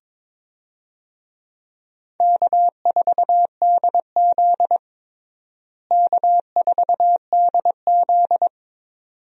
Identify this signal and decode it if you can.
Morse code, heard as a keyed tone, about 22 wpm, 700 Hz. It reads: K4DZ K4DZ
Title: USA Call Signs - 4 Characters - Repeated 2x - 22wpm